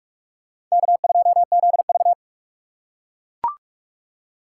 Morse code, heard as a keyed tone, about 38 wpm, 700 Hz. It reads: K2ZV